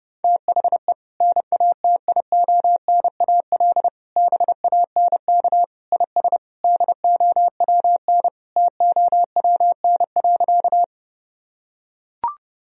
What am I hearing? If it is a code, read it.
Morse code, heard as a keyed tone, about 30 wpm, 700 Hz. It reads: THE NATIONAL BANK IS DOWN TOWN.
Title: Sentences from Top 1000 Words - Review Part 4 - 30wpm